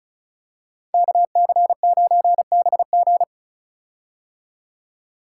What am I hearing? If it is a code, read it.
Morse code, heard as a keyed tone, about 36 wpm, 700 Hz. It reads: KC9BG
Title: K1USN SST QSO Practice - 35wpm